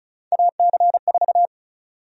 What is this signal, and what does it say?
Morse code, heard as a keyed tone, about 34 wpm, 700 Hz. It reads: AC4
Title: USA Call Sign Prefixes - Repeated 2x - 35wpm